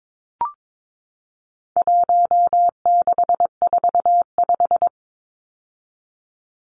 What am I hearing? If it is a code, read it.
Morse code, heard as a keyed tone, about 22 wpm, 700 Hz. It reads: 1645
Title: Numbers - 4-Digit 22wpm